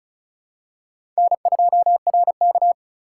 Morse code, heard as a keyed tone, about 34 wpm, 700 Hz. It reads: N2RK
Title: USA Call Signs - 4 Characters - 3x Character Spacing - 35wpm